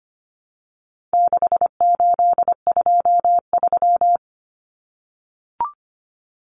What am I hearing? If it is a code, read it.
Morse code, heard as a keyed tone, about 25 wpm, 700 Hz. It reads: 6823
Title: Numbers - 4-Digit 25wpm